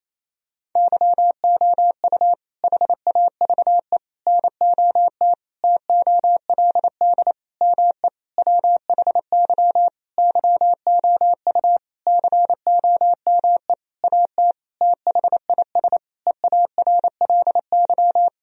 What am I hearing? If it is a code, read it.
Morse code, heard as a keyed tone, about 28 wpm, 700 Hz. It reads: YOU HAVE NOT TOLD ME WHY YOU COME AT THIS EARLY
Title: Sentences from Top 1000 Words - Review Part 2 - 28wpm